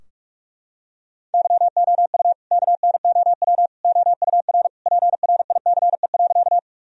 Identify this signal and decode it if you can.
Morse code, heard as a keyed tone, about 47 wpm, 700 Hz. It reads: YOU KNOW OUR PRICE.
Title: Sentences from Top 1000 Words - Review Part 4 - 45wpm